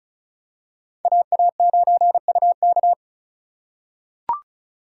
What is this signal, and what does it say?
Morse code, heard as a keyed tone, about 36 wpm, 700 Hz. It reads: AA9UK
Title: USA Call Signs - 5 Characters - 6x Character Spacing - 35wpm